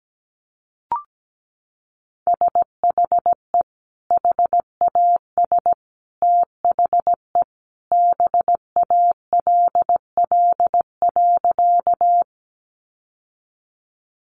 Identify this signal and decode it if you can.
Morse code, heard as a keyed tone, about 17 wpm, 700 Hz. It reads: SHE HAS THE BALL.
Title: Sentences from Top 900 Words 17wpm